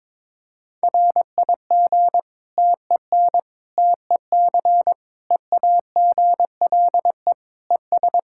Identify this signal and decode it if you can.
Morse code, heard as a keyed tone, about 22 wpm, 700 Hz. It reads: RIG TEN TEC EAGLE ES
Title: Practice QSO - Exchange 2 - Info - 22wpm